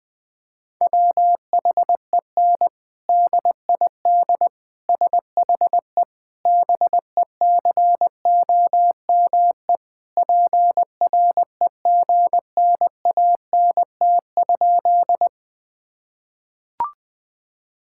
Morse code, heard as a keyed tone, about 20 wpm, 700 Hz. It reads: WHEN DID SHE BECOME PREGNANT?
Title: Sentences from Top 1000 Words - Review Part 6 - 20wpm